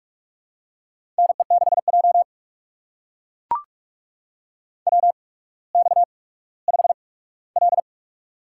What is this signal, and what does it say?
Morse code, heard as a keyed tone, about 44 wpm, spaced as for 19 wpm, 700 Hz. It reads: NE6J WX5L
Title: USA Call Signs - 4 Characters - 8x Character Spacing - 45wpm